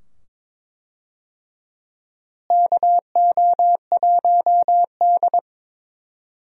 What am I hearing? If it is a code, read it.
Morse code, heard as a keyed tone, about 22 wpm, 700 Hz. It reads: KO1D
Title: USA Call Signs - 4 Characters - Repeated 2x - 22wpm